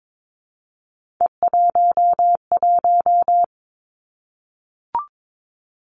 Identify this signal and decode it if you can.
Morse code, heard as a keyed tone, about 22 wpm, 700 Hz. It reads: E11
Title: Three Challenging Letter-Number Combinations 22wpm